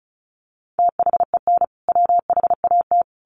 Morse code, heard as a keyed tone, about 34 wpm, 700 Hz. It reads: THEN WHAT
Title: Sentences from Top 100 Words 35wpm